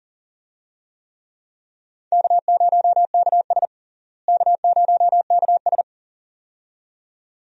Morse code, heard as a keyed tone, about 40 wpm, 700 Hz. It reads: K0KS K0KS
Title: USA Call Signs - 4 Characters - Repeated 2x - 40wpm